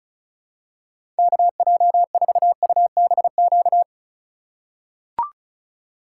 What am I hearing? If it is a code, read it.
Morse code, heard as a keyed tone, about 35 wpm, 690 Hz. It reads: KJ4UBQ
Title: USA Call Signs - 6 Characters - 3x Character Spacing - 35wpm